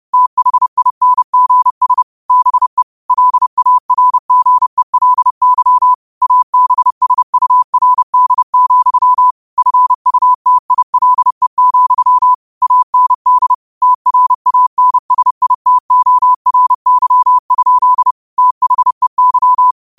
Morse code, heard as a keyed tone, about 30 wpm, 1000 Hz. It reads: THINGS BE LARGELY ABSURD, FUTILE, AND TRANSITORY? THEY